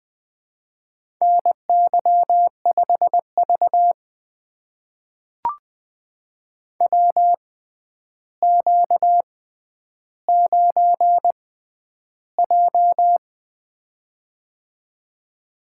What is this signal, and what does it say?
Morse code, heard as a keyed tone, about 20 wpm, spaced as for 10 wpm, 700 Hz. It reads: NY5V WQ9J
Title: USA Call Signs - 4 Characters - 6x Character Spacing - 20wpm